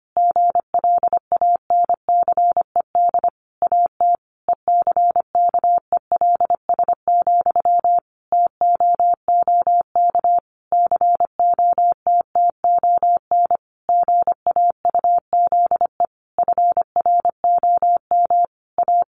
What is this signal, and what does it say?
Morse code, heard as a keyed tone, about 25 wpm, 700 Hz. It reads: GLANCED AT ECKELS, TOOK COTTON GAUZE FROM A
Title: A Sound Of Thunder 25wpm